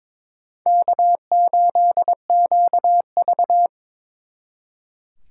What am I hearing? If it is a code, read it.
Morse code, heard as a keyed tone, about 26 wpm, 700 Hz. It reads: K8QV
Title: Speed Racing - USA 1x2 Call Signs 22wpm